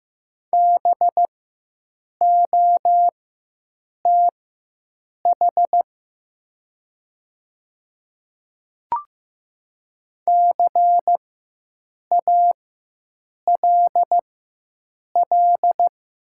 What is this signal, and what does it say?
Morse code, heard as a keyed tone, about 15 wpm, spaced as for 7 wpm, 700 Hz. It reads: BOTH CALL
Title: 4x Character Spacing - Top 300 Words - 15wpm